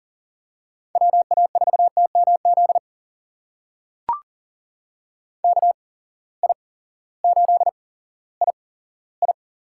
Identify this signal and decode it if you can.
Morse code, heard as a keyed tone, about 39 wpm, spaced as for 18 wpm, 700 Hz. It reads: WA4TMZ KI8II
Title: USA Call Signs - 8x Character Spacing - 40wpm